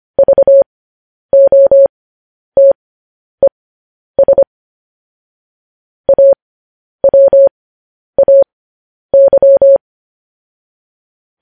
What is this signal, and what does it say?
Morse code, heard as a keyed tone, about 25 wpm, spaced as for 10 wpm, 550 Hz. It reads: VOTES AWAY